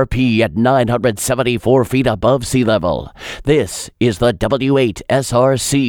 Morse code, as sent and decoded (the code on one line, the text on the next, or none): none